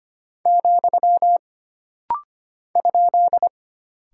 Morse code, heard as a keyed tone, about 25 wpm, 700 Hz. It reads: , ?
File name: Single Punctuation-Prosign - Rapid-Fire - 25wpm